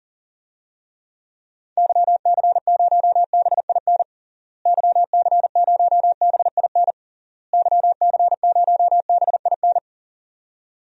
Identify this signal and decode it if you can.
Morse code, heard as a keyed tone, about 37 wpm, 700 Hz. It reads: YC0BIN YC0BIN YC0BIN
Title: DXCC-100 Call Signs - 6 Characters - Repeated 3x - 40wpm